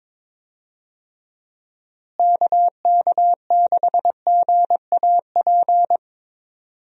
Morse code, heard as a keyed tone, about 22 wpm, 700 Hz. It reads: KK6GAP
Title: USA Call Signs - 6 Characters - Repeated 3x - 22wpm